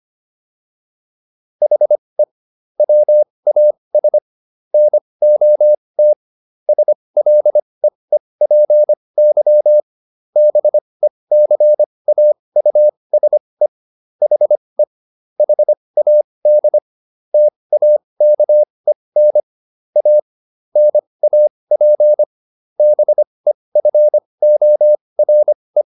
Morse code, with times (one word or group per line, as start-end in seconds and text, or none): 1.6-2.2 HE
2.8-4.2 WAS
4.7-6.1 NOT
6.7-9.8 SLEEPY
10.4-13.7 BECAUSE
14.2-14.8 HE
15.4-16.8 HAD
17.3-19.4 TAKEN
20.0-20.2 A
20.8-22.2 NAP
22.8-25.8 BEFORE